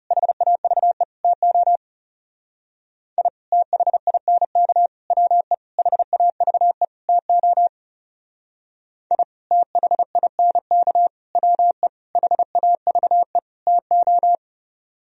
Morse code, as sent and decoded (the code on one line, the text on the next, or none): .... .- ...- . / - --- / .. / - .... .. -. -.- / .-- . / .... .- ...- . / - --- / .. / - .... .. -. -.- / .-- . / .... .- ...- . / - ---
HAVE TO I THINK WE HAVE TO I THINK WE HAVE TO